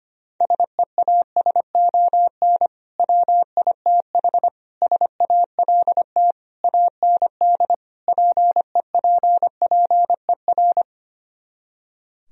Morse code, heard as a keyed tone, about 24 wpm, 700 Hz. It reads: SEASON WITH SALT AND PEPPER
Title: Speed-Racing - Sets of 5 Words 25wpm